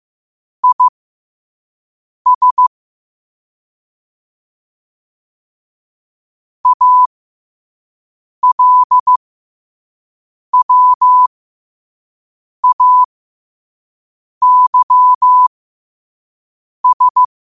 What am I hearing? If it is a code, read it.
Morse code, heard as a keyed tone, about 15 wpm, spaced as for 5 wpm, 1000 Hz. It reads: IS ALWAYS
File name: qotdfiles / 5wpm